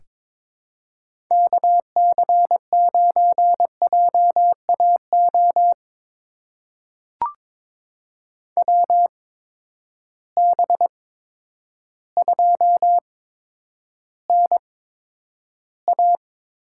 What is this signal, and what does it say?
Morse code, heard as a keyed tone, about 22 wpm, spaced as for 10 wpm, 700 Hz. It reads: KC9JAO WB2NA
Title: USA Call Signs - 6 Characters - 8x Character Spacing - 22wpm